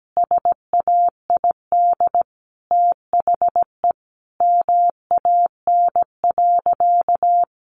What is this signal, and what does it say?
Morse code, heard as a keyed tone, about 17 wpm, 700 Hz. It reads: SAID THE MAN.
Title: Sentences from Top 200 Words 17wpm